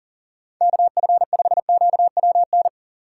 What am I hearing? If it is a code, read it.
Morse code, heard as a keyed tone, about 38 wpm, 700 Hz. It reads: KF5QWN
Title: USA Call Signs - 6 Characters - 3x Character Spacing - 40wpm